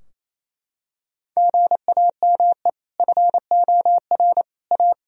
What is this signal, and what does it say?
Morse code, heard as a keyed tone, about 28 wpm, 700 Hz. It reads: GAME FOR A
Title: Sentences from Top 1000 Words - Review Part 3 - 28wpm